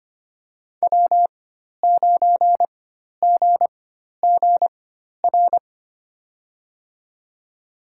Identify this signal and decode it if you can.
Morse code, heard as a keyed tone, about 25 wpm, spaced as for 12 wpm, 700 Hz. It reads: W9GGR
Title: USA Call Signs - 5 Characters - 4x Character Spacing - 25wpm